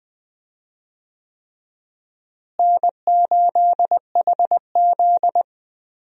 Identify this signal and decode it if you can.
Morse code, heard as a keyed tone, about 20 wpm, 700 Hz. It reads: N8HZ
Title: USA Call Signs - 4 Characters - Repeated 3x - 20wpm